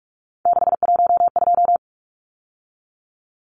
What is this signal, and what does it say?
Morse code, heard as a keyed tone, about 44 wpm, 700 Hz. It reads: 612
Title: Numbers - 3-Digit 45wpm